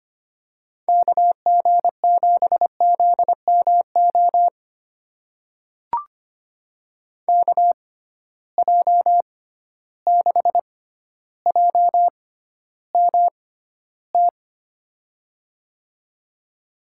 Morse code, heard as a keyed tone, about 25 wpm, spaced as for 13 wpm, 700 Hz. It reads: KG7ZMO KJ6JMT
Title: USA Call Signs - 6 Characters - 6x Character Spacing - 25wpm